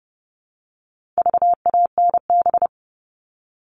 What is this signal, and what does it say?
Morse code, heard as a keyed tone, about 30 wpm, 700 Hz. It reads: VANB
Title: All Cut Numbers - 4-Digit 30wpm